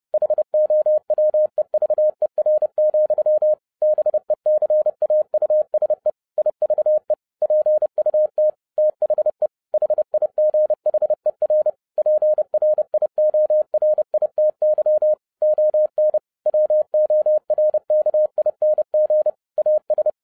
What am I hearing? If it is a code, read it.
Morse code, heard as a keyed tone, about 30 wpm, 610 Hz. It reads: HOWEVER, BECAUSE IVE PUT THE HIGHER PRIORITY ON WORKING AS
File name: 2012-01-mp3 / n11-I DIDNT MISS THE SOLAR CYCLE 24-30wpm